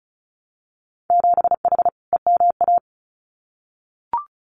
Morse code, heard as a keyed tone, about 36 wpm, 700 Hz. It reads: 7H EMA
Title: Winter Field Day Exchanges Encore 35wpm